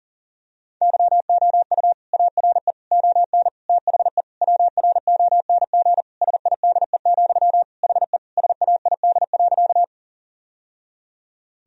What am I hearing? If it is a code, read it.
Morse code, heard as a keyed tone, about 40 wpm, 700 Hz. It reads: YOU ARE ON THE WRONG SIDE, HE SAID.
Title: Sentences from Top 1000 Words - Review Part 1 - 40wpm